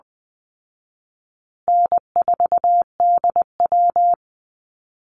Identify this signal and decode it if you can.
Morse code, heard as a keyed tone, about 20 wpm, 700 Hz. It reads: N4DW
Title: Call Signs Repeated 2x - 20wpm